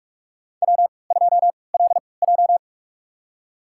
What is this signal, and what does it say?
Morse code, heard as a keyed tone, about 44 wpm, spaced as for 25 wpm, 700 Hz. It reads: W2LJ
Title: USA Call Signs - 4 Characters - 3x Character Spacing - 45wpm